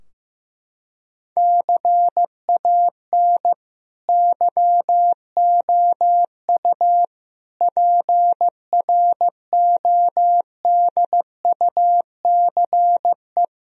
Morse code, heard as a keyed tone, about 15 wpm, 700 Hz. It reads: CAN YOU PRODUCE
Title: Sentences from Top 1000 Words - Review Part 3 - 15wpm